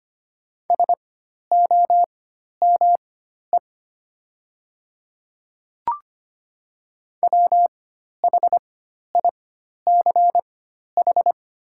Morse code, heard as a keyed tone, about 25 wpm, spaced as for 12 wpm, 700 Hz. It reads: SOME WHICH
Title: 4x Character Spacing - Top 100 Words - 25wpm